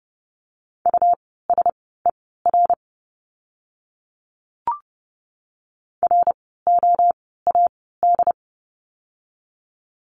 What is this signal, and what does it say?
Morse code, heard as a keyed tone, about 30 wpm, spaced as for 17 wpm, 700 Hz. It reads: USER ROAD
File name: Lesson 17f - Letter u - Words 30wpm